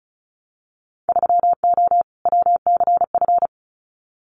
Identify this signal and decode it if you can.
Morse code, heard as a keyed tone, about 34 wpm, 700 Hz. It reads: 3O WCF
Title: Winter Field Day Exchanges 35wpm